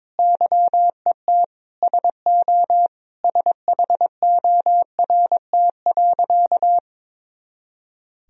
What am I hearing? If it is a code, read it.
Morse code, heard as a keyed tone, about 22 wpm, 700 Hz. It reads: YET SO SHORT.
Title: Sentences from Top 500 Words and Abbreviations 22wpm